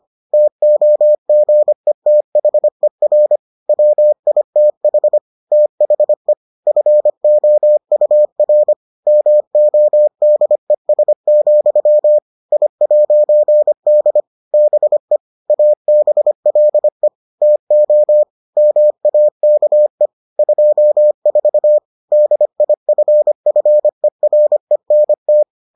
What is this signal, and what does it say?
Morse code, heard as a keyed tone, about 25 wpm, 600 Hz. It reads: TOGETHER WITH THE FOUR MODES, I'D BE ABLE TO MAKE 24 DIFFERENT